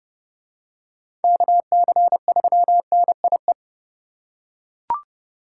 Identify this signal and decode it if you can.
Morse code, heard as a keyed tone, about 30 wpm, 700 Hz. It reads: KC3NIE
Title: K1USN SST QSO Practice - 30wpm